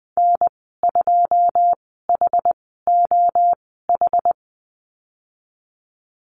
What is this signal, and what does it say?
Morse code, heard as a keyed tone, about 20 wpm, spaced as for 15 wpm, 700 Hz. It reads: N2HOH
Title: 088 Lesson 15g - Number 5 - Call Signs 20wpm